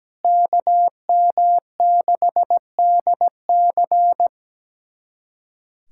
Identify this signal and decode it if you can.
Morse code, heard as a keyed tone, about 17 wpm, 700 Hz. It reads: KM6DC
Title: USA Callsign DE Callsign - 5 Characters - 17wpm